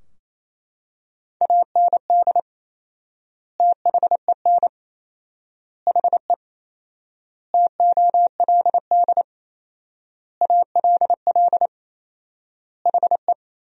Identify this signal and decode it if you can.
Morse code, heard as a keyed tone, about 27 wpm, 700 Hz. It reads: AND THEN HE TOLD ALL HE
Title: 4x Word-Spacing - Sentences from Top 400 Words - 28wpm